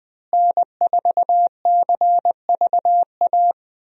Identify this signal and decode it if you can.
Morse code, heard as a keyed tone, about 20 wpm, 700 Hz. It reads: N4CVA
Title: USA Call Signs - 5 Characters - Repeated 3x - 20wpm